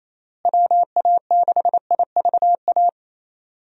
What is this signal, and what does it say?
Morse code, heard as a keyed tone, about 28 wpm, 700 Hz. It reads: WA6IVA
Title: USA Call Signs - 6 Characters - Repeated 3x - 28wpm